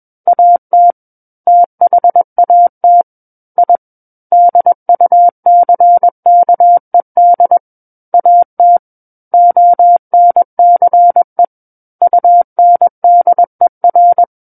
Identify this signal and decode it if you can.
Morse code, heard as a keyed tone, about 21 wpm, 700 Hz. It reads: AT THAT I DUCKED AT ONCE UNDER
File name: Book One / 12-Chapter 12 - 21 WPM